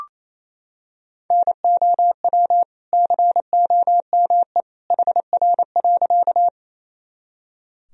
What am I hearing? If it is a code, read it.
Morse code, heard as a keyed tone, about 28 wpm, 700 Hz. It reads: NOW COME HR.
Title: Sentences from Top 200 Words and Abbreviations 28wpm